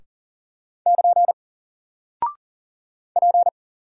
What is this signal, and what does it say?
Morse code, heard as a keyed tone, about 40 wpm, 700 Hz. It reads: <KN> P
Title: Single Character (Letter:Number:Punctuation:Prosign) 40wpm